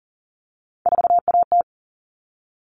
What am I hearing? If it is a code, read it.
Morse code, heard as a keyed tone, about 40 wpm, 700 Hz. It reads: 4AT
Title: All Cut Numbers - 3-Digit 40wpm